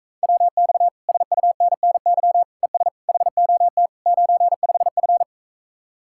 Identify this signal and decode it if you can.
Morse code, heard as a keyed tone, about 45 wpm, 700 Hz. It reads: WX SUNNY ES HOT 95F